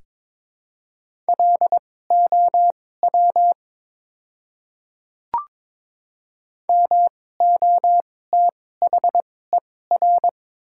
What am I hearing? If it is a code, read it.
Morse code, heard as a keyed tone, about 22 wpm, spaced as for 16 wpm, 700 Hz. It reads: LOW MOTHER
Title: 2x Character Spacing - Top 200 Words - 22wpm